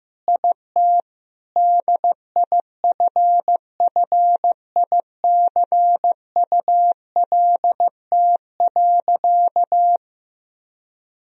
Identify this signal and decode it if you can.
Morse code, heard as a keyed tone, about 15 wpm, 700 Hz. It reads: IT DIFFICULT.
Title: Sentences from Top 1000 Words - Review Part 3 - 15wpm